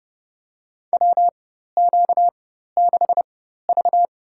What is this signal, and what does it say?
Morse code, heard as a keyed tone, about 30 wpm, spaced as for 14 wpm, 700 Hz. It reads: WQ6V
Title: USA Call Signs - 4 Characters - 4x Character Spacing - 30wpm